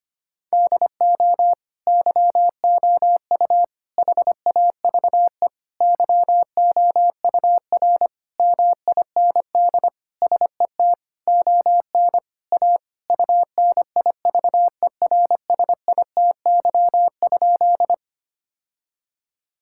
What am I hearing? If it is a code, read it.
Morse code, heard as a keyed tone, about 25 wpm, 700 Hz. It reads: DO YOU HAVE YOUR MIND SET ON A UNIVERSITY?
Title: Sentences from Top 1000 Words - Review Part 5 - 25wpm